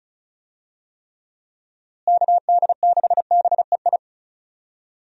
Morse code, heard as a keyed tone, about 34 wpm, 700 Hz. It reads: KD6BEI
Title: USA Call Signs - 6 Characters - Repeated 3x - 35wpm